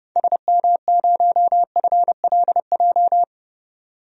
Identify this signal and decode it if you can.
Morse code, heard as a keyed tone, about 30 wpm, 700 Hz. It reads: SM0FLJ
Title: DXCC-100 Call Signs - 6 Characters - 30wpm